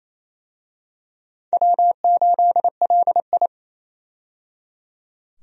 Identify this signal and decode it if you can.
Morse code, heard as a keyed tone, about 27 wpm, 700 Hz. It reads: W8LI
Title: USA Call Signs - 4 Characters - 28wpm